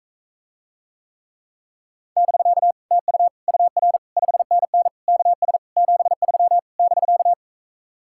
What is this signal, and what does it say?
Morse code, heard as a keyed tone, about 44 wpm, 700 Hz. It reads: <BK> TU UR 5NN KS 73 <BK>